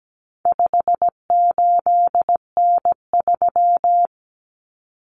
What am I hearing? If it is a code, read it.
Morse code, heard as a keyed tone, about 17 wpm, 700 Hz. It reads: 58N3
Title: Common Cut Numbers - 4-Digit 17wpm